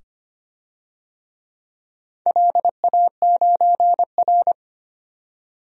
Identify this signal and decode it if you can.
Morse code, heard as a keyed tone, about 25 wpm, 700 Hz. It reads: LA9R
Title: DXCC-100 Call Signs - 4 Characters - 25wpm